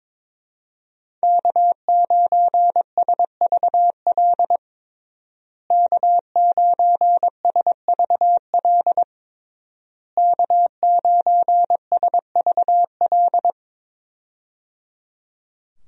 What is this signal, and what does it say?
Morse code, heard as a keyed tone, about 22 wpm, 700 Hz. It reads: K9SVL K9SVL K9SVL
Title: USA Call Signs - 5 Characters - Repeated 3x - 22wpm